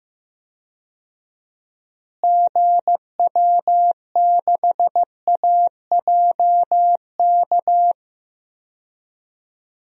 Morse code, heard as a keyed tone, about 15 wpm, 700 Hz. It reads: GW6AJK
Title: DXCC-100 Call Signs - 6 Characters - Repeated 2x - 15wpm